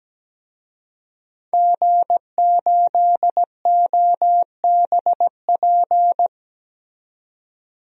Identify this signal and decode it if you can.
Morse code, heard as a keyed tone, about 17 wpm, 700 Hz. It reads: G8OBP
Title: DXCC-100 Call Signs - 5 Characters - 17wpm